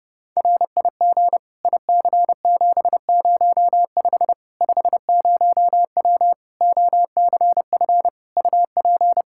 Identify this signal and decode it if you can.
Morse code, heard as a keyed tone, about 30 wpm, 700 Hz. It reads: RIG IC705 50W OCF UP
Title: Practice QSO - Exchange 2 - Info - 30wpm